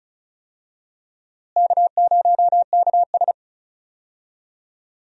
Morse code, heard as a keyed tone, about 36 wpm, 700 Hz. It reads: K0KS
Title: USA Call Signs - 4 Characters - Repeated 2x - 35wpm